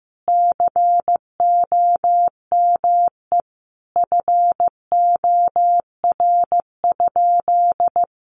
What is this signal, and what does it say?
Morse code, heard as a keyed tone, about 15 wpm, 700 Hz. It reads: COME FOR?
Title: Sentences from Top 100 Words 15wpm